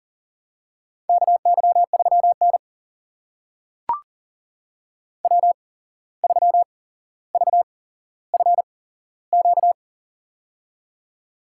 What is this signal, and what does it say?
Morse code, heard as a keyed tone, about 40 wpm, spaced as for 16 wpm, 700 Hz. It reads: KY3N W3VFQ
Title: USA Call Signs - 8x Character Spacing - 40wpm